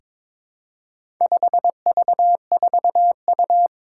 Morse code, heard as a keyed tone, about 22 wpm, 700 Hz. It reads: 5V4U
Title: Four Challenging Letter-Number Combinations 22wpm